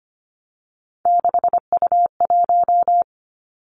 Morse code, heard as a keyed tone, about 25 wpm, 700 Hz. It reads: 6U1
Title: Three Challenging Letter-Number Combinations 25wpm